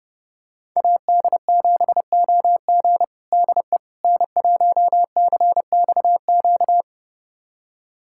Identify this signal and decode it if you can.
Morse code, heard as a keyed tone, about 32 wpm, 700 Hz. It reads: AD7OG DE N1CXQ
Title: USA Callsign DE Callsign - 5 Characters - 30wpm